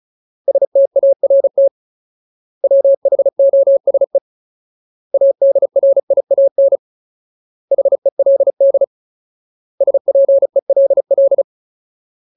Morse code, heard as a keyed tone, about 34 wpm, 550 Hz. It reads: START WHOSE ADRIAN HELD SPELL
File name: Ultimate Word List / 35WPM Ultimate